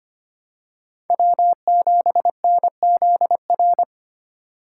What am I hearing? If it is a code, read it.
Morse code, heard as a keyed tone, about 25 wpm, 700 Hz. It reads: W7NZR
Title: USA Call Signs - 5 Characters - 4x Character Spacing - 25wpm